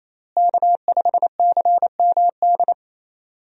Morse code, heard as a keyed tone, about 28 wpm, 700 Hz. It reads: K5CMD
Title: USA Call Signs - 5 Characters - Repeated 2x - 28wpm